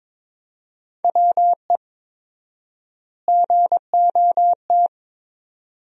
Morse code, heard as a keyed tone, about 22 wpm, 700 Hz. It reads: WE GOT
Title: 4x Word-Spacing - Sentences from Top 400 Words - 22wpm